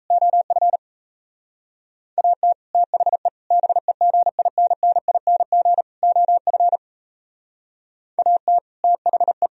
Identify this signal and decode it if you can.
Morse code, heard as a keyed tone, about 34 wpm, 700 Hz. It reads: OF AT THE BEGINNING OF AT THE
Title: Speed-Racing - Sets of 4 Words 28wpm